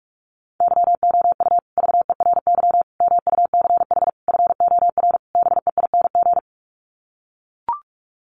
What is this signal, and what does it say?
Morse code, heard as a keyed tone, about 44 wpm, 700 Hz. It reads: YOU VERY MUCH FOR BEING
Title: Sets of 5 Words Encore 45wpm